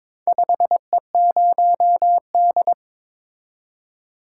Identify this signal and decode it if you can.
Morse code, heard as a keyed tone, about 23 wpm, 700 Hz. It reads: 5E0D
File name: practice / 22wpm / Four Letter-Number Combinations - 22wpm